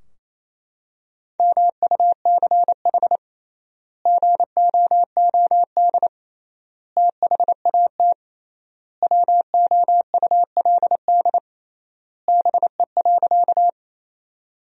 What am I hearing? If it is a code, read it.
Morse code, heard as a keyed tone, about 28 wpm, 700 Hz. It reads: MUCH GOOD THAT WOULD BE.
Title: 3x Word-Spacing - Sentences from Top 200 Words - 28wpm